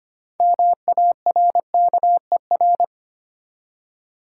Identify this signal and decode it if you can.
Morse code, heard as a keyed tone, about 25 wpm, 700 Hz. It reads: MARKER
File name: practice / 25wpm / Easy Words with Hints - 25wpm